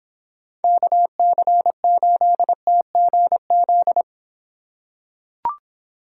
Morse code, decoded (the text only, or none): KC8TGZ